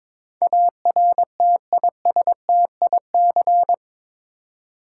Morse code, heard as a keyed tone, about 22 wpm, 700 Hz. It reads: ARTISTIC
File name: practice / 22wpm / Eight Letter Words - 22wpm